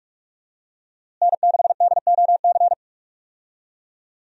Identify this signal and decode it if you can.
Morse code, heard as a keyed tone, about 42 wpm, 700 Hz. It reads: N6DOC
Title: USA Call Signs - 5 Characters - Repeated 2x - 45wpm